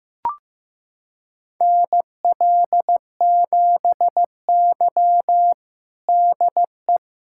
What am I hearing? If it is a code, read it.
Morse code, heard as a keyed tone, about 16 wpm, 700 Hz. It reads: NL7Y DE
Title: USA Callsign DE Callsign - 4 Characters - 15wpm